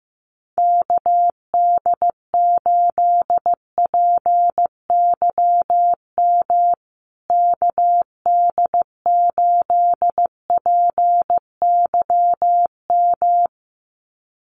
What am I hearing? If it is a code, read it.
Morse code, heard as a keyed tone, about 15 wpm, 700 Hz. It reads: KD8PYM KD8PYM
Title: USA Call Signs Repeated 2x 15wpm